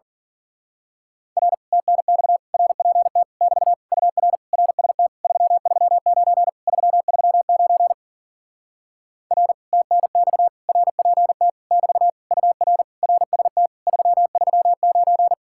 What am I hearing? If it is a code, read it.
Morse code, heard as a keyed tone, about 47 wpm, 700 Hz. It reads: R TNX RPT <BT> UR RST 339 339 R TNX RPT <BT> UR RST 339
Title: Speed-Racing Common QSO Phrases - 40wpm